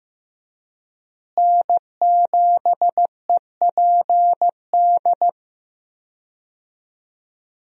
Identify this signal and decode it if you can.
Morse code, heard as a keyed tone, about 15 wpm, 700 Hz. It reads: N7EPD
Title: USA Call Signs - 5 Characters - Repeated 2x - 15wpm